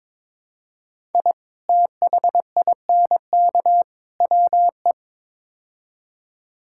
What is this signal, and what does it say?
Morse code, heard as a keyed tone, about 22 wpm, 700 Hz. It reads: I THINK WE
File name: practice / 22wpm / Build A Phrase - 5-Word Phrases - 22wpm